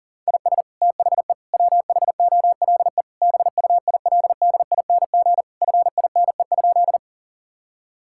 Morse code, heard as a keyed tone, about 37 wpm, 700 Hz. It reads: IS THE WHOLE BUILDING FINE?
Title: Sentences from Top 1000 Words - Review Part 5 - 40wpm